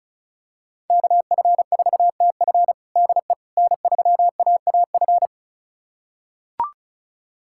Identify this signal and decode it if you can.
Morse code, heard as a keyed tone, about 36 wpm, 700 Hz. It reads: KF4TF DE N3AAF
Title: USA Callsign DE Callsign - 5 Characters - 35wpm